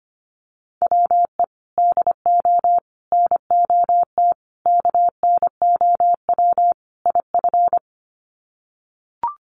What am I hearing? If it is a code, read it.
Morse code, heard as a keyed tone, about 25 wpm, 700 Hz. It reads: WE DO NOT KNOW IF
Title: Sets of 5 Words Encore2 25wpm